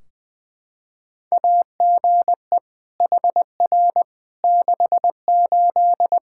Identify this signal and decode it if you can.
Morse code, heard as a keyed tone, about 21 wpm, 700 Hz. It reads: AGE HR 68
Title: K1USN SST QSO Practice 20wpm